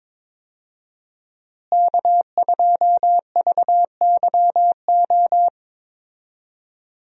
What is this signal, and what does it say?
Morse code, heard as a keyed tone, about 22 wpm, 700 Hz. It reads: K2VYO
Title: USA Call Signs - 5 Characters - Repeated 3x - 22wpm